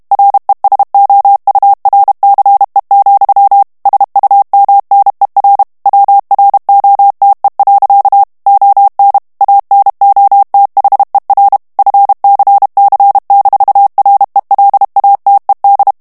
Morse code, heard as a keyed tone, about 32 wpm, 800 Hz. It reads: RESOURCE, SUMNER WROTE. ON ANOTHER FCC-RELATED
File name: morse mp3 / ANGLAIS texte clair 3 vitesse 30 mots minute